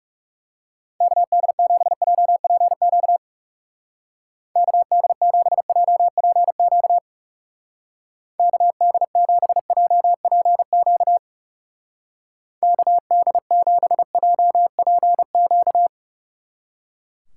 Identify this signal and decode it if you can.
Morse code, heard as a keyed tone, about 37 wpm, 700 Hz. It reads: KD7JPQ KD7JPQ KD7JPQ KD7JPQ
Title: Speed Racing - USA 2x3 Call Signs 30wpm